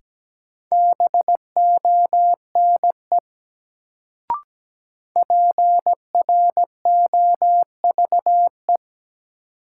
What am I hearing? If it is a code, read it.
Morse code, heard as a keyed tone, about 17 wpm, 700 Hz. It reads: BONE PROVE
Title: Sentences from Top 1600 Words - New Word Drill - 17wpm